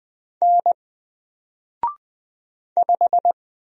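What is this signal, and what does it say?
Morse code, heard as a keyed tone, about 20 wpm, 700 Hz. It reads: N 5
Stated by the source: Single letter and number Morse drill at 20 wpm, from N4LCD.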